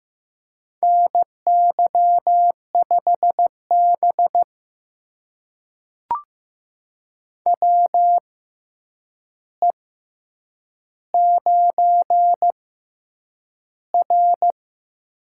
Morse code, heard as a keyed tone, about 15 wpm, spaced as for 8 wpm, 700 Hz. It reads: NY5B WE9R
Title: USA Call Signs - 4 Characters - 6x Character Spacing - 15wpm